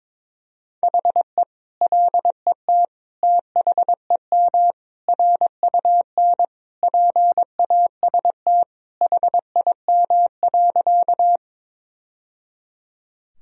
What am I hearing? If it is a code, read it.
Morse code, heard as a keyed tone, about 22 wpm, 700 Hz. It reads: HE LET THEM RUN PAST HIM.
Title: Sentences from Top 1000 Words - Review Part 3 - 22wpm